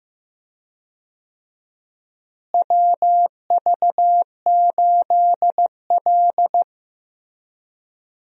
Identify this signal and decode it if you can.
Morse code, heard as a keyed tone, about 15 wpm, 700 Hz. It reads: WV8L
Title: USA Call Signs - 4 Characters - Repeated 2x - 15wpm